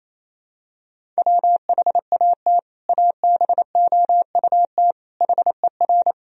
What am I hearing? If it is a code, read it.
Morse code, heard as a keyed tone, about 28 wpm, 700 Hz. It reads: WHAT ABOUT HER
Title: Sentences from Top 1000 Words - Review Part 1 - 28wpm